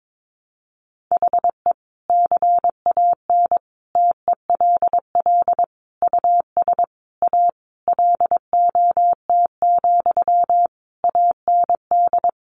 Morse code, heard as a keyed tone, about 22 wpm, 700 Hz. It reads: HE CAN TELL US A LOT, AND
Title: Sentences from Top 1300 Words 22wpm